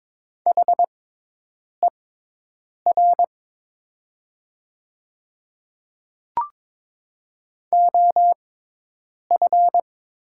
Morse code, heard as a keyed tone, about 22 wpm, spaced as for 8 wpm, 700 Hz. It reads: HER OF
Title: 6x Character Spacing - Top 100 Words - 22wpm